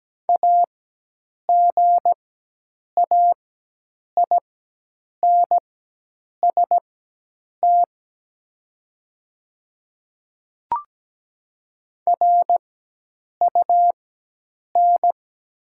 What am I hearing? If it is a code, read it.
Morse code, heard as a keyed tone, about 17 wpm, spaced as for 8 wpm, 700 Hz. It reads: AGAINST RUN
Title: 4x Character Spacing - Top 500 Words - 17wpm